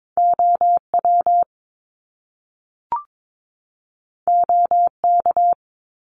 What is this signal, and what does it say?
Morse code, heard as a keyed tone, about 22 wpm, 700 Hz. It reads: OW OK
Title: Top Two Letter Combinations in Words 22wpm